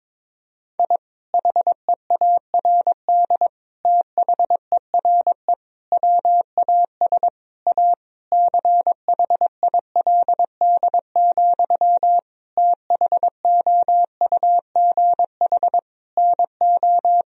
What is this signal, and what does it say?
Morse code, heard as a keyed tone, about 22 wpm, 700 Hz. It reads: I HEARD THERE WAS A CHILD, THOUGH NO
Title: Sentences from Top 1000 Words - Review Part 2 - 22wpm